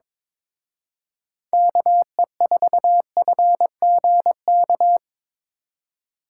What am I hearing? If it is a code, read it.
Morse code, heard as a keyed tone, about 25 wpm, 700 Hz. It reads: KE4FGK
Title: Speed Racing - USA 2x3 Call Signs 22wpm